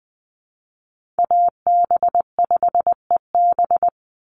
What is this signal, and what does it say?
Morse code, heard as a keyed tone, about 20 wpm, 700 Hz. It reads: AB5EB
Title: Call Signs - 20wpm